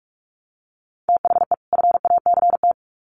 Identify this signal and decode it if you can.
Morse code, heard as a keyed tone, about 44 wpm, 700 Hz. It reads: THE FACT
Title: Sets of 2 Words 45wpm